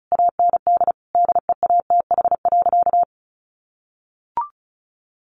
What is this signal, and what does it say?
Morse code, heard as a keyed tone, about 34 wpm, 700 Hz. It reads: AND DEATH.
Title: Sentences from Top 1100 Words 35wpm